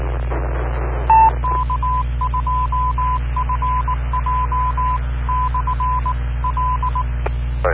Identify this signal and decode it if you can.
Morse code, heard as a keyed tone, about 19 wpm, 1010 Hz, with noise. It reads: K2FJ/L